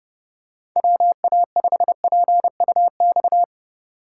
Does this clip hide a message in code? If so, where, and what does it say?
Yes: Morse code from 0.8 to 3.4 s, heard as a keyed tone, about 30 wpm, 700 Hz. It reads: WA5PUX